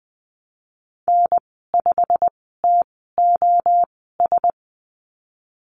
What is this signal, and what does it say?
Morse code, heard as a keyed tone, about 20 wpm, spaced as for 15 wpm, 700 Hz. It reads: N5TOS
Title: 128 Lesson 21g - Number 6 - Call Signs 20wpm